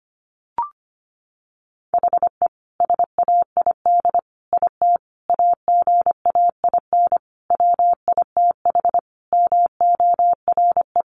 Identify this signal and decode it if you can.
Morse code, heard as a keyed tone, about 25 wpm, 700 Hz. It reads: HE SAID IT AGAIN WITH MORE
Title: Sentences from Top 1400 Words 25wpm